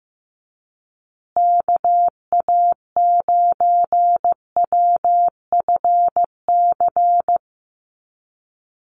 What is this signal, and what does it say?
Morse code, heard as a keyed tone, about 15 wpm, 700 Hz. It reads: KA9WFC
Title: USA Call Signs Repeated 2x 15wpm